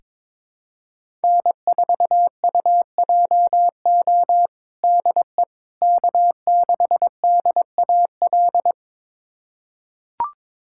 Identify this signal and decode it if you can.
Morse code, heard as a keyed tone, about 21 wpm, 700 Hz. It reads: N4UJO DE K6DAL
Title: USA Callsign DE Callsign - 5 Characters - 22wpm